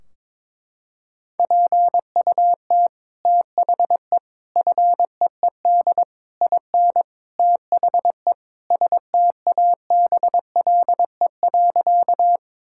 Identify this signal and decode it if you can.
Morse code, heard as a keyed tone, about 22 wpm, 700 Hz. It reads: PUT THE FEED IN THE STABLE.
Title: Sentences from Top 1600 Words 22wpm